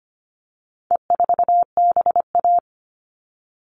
Morse code, heard as a keyed tone, about 25 wpm, 700 Hz. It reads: E4BA
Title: All Cut Numbers - 4-Digit 25wpm